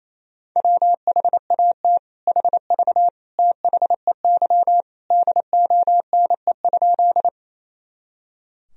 Morse code, heard as a keyed tone, about 28 wpm, 700 Hz. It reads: WHAT HV THEY DONE?
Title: Sentences from Top 500 Words and Abbreviations 28wpm